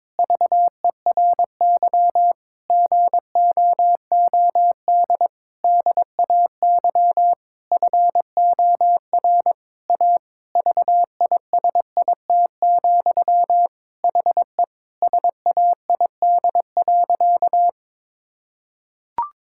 Morse code, heard as a keyed tone, about 21 wpm, 700 Hz. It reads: VERY GOOD DAY FOR A VISIT, HE SAID.
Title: Sentences from Top 1000 Words - Review Part 6 - 22wpm